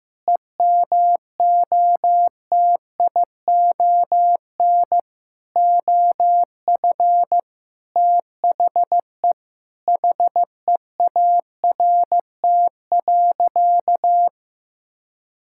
Morse code, heard as a keyed tone, about 16 wpm, 700 Hz. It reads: EMOTION OF THE HEART.
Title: Sentences from Top 1600 Words 15wpm